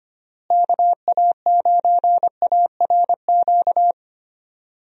Morse code, heard as a keyed tone, about 25 wpm, 700 Hz. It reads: KA9ARQ
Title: USA Call Signs - 6 Characters - Repeated 2x - 25wpm